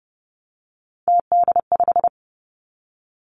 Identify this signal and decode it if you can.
Morse code, heard as a keyed tone, about 30 wpm, 700 Hz. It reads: TD5
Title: Three Challenging Letter-Number Combinations 30wpm